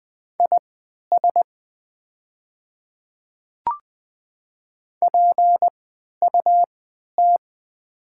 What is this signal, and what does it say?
Morse code, heard as a keyed tone, about 20 wpm, spaced as for 11 wpm, 700 Hz. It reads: IS PUT
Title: 3x Character Spacing - Top 300 Words - 20wpm